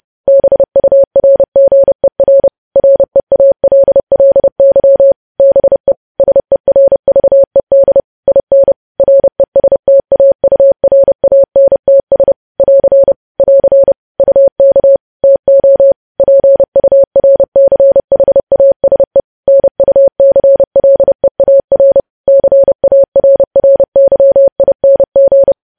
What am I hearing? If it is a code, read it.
Morse code, heard as a keyed tone, about 30 wpm, 550 Hz. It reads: BURGER REALLY BE SERVED IN RESTAURANTS <AR> <AR> UK TO PURCHASE NUCLEAR CARRYING
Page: News Headlines in Morse Code at 30 WPM